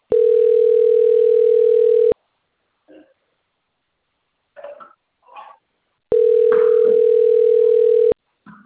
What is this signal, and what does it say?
Ringback tone (North America), 440 + 480 Hz.
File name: out- - - - - .13539